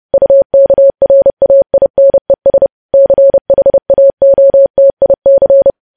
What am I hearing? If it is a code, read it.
Morse code, heard as a keyed tone, about 30 wpm, 550 Hz. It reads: UKRAINES CHAOTIC
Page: news headlines in morse code at 30 wpm